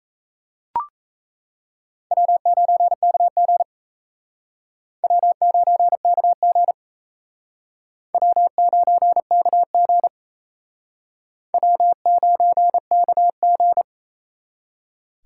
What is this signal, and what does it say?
Morse code, heard as a keyed tone, about 38 wpm, 700 Hz. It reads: W9KG W9KG W9KG W9KG